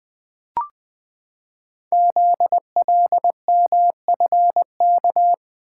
Morse code, heard as a keyed tone, about 20 wpm, 700 Hz. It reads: ZLMFK
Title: Code Groups - 5 Letters - 20wpm